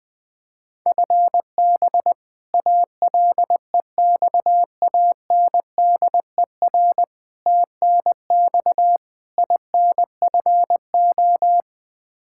Morse code, heard as a keyed tone, about 20 wpm, 700 Hz. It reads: FB ALEXANDER TNX INFO
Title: K1USN SST QSO Practice 20wpm